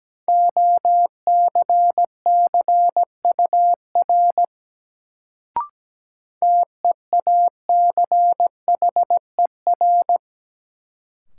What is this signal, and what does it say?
Morse code, heard as a keyed tone, about 17 wpm, 700 Hz. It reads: OCCUR TEACHER
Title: Sentences from Top 900 Words - New Word Drill - 17wpm